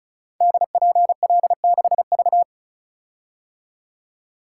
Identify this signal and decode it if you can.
Morse code, heard as a keyed tone, about 36 wpm, 700 Hz. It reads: DPL6V
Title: Code Groups - 5 Letter:Numbers - 35wpm